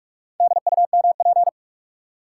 Morse code, heard as a keyed tone, about 47 wpm, 700 Hz. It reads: DUMP
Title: Four Letter Words - 45wpm